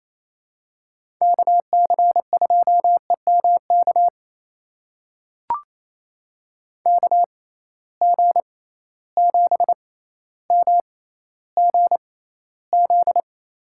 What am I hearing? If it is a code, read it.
Morse code, heard as a keyed tone, about 27 wpm, spaced as for 14 wpm, 700 Hz. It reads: KC2EMK KG7MGZ
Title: USA Call Signs - 6 Characters - 6x Character Spacing - 28wpm